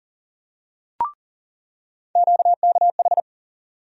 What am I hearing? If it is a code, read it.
Morse code, heard as a keyed tone, about 40 wpm, 700 Hz. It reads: QKH
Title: USA Call Sign Postfixes - 3 Letters 40wpm